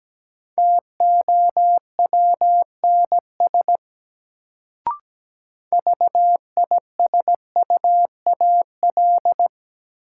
Morse code, heard as a keyed tone, about 17 wpm, 700 Hz. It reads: TOWNS VISUAL
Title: Sentences from Top 1000 Words - Review Part 6 - New Word Drill - 17wpm